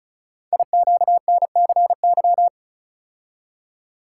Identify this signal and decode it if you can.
Morse code, heard as a keyed tone, about 36 wpm, 700 Hz. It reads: IQNCY
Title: Code Groups - 5 Letters - 35wpm